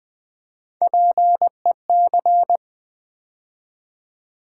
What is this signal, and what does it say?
Morse code, heard as a keyed tone, about 20 wpm, 700 Hz. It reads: PEC
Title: Build A Word - 3 Letters At A Time - 8-Letter Words - 20wpm